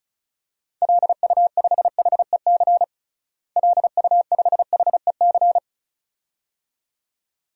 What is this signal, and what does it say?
Morse code, heard as a keyed tone, about 34 wpm, 700 Hz. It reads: LU5HEC LU5HEC
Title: DXCC-100 Call Signs - 6 Characters - Repeated 2x - 35wpm